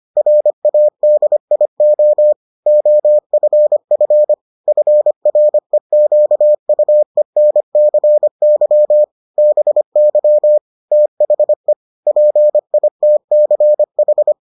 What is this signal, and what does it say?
Morse code, heard as a keyed tone, about 25 wpm, 600 Hz. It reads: RADIO OFF FREQUENCY BY THE PITCH